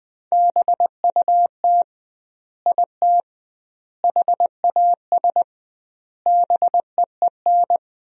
Morse code, heard as a keyed tone, about 21 wpm, 700 Hz. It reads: BUT IT HAS BEEN
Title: 2x Word-Spacing - Sentences from Top 100 Words - 20wpm